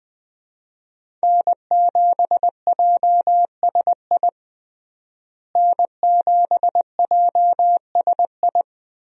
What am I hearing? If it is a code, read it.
Morse code, heard as a keyed tone, about 20 wpm, 700 Hz. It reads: N7JSI N7JSI
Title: USA Call Signs - 5 Characters - Repeated 2x - 20wpm